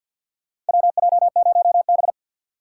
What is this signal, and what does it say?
Morse code, heard as a keyed tone, about 51 wpm, 700 Hz. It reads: WJ0B
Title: USA Call Signs - 4x Character Spacing - 50wpm